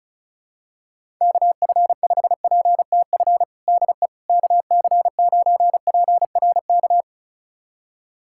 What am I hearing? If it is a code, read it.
Morse code, heard as a keyed tone, about 36 wpm, 700 Hz. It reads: KF5PTF DE KC9PRK
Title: USA Callsign DE Callsign - 6 Characters - 35wpm